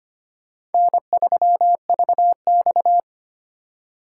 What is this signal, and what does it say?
Morse code, heard as a keyed tone, about 32 wpm, 700 Hz. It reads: N3VX